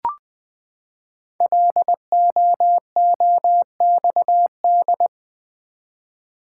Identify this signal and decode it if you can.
Morse code, heard as a keyed tone, about 20 wpm, 700 Hz. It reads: LOOXD
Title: Code Groups - 5 Letter:Numbers - 20wpm